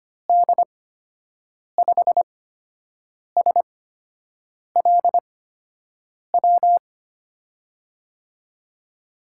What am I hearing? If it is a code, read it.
Morse code, heard as a keyed tone, about 25 wpm, spaced as for 7 wpm, 700 Hz. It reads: D5SLW
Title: USA Call Signs - 6 Characters - 8x Character Spacing - 25wpm